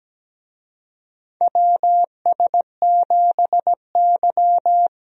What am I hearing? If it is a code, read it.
Morse code, heard as a keyed tone, about 17 wpm, 700 Hz. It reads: WS7Y
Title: USA Call Signs - 4 Characters - Repeated 2x - 17wpm